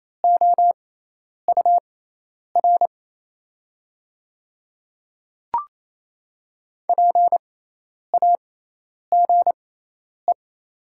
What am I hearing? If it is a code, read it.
Morse code, heard as a keyed tone, about 28 wpm, spaced as for 10 wpm, 700 Hz. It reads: OUR PAGE
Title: 6x Character Spacing - Top 300 Words - 28wpm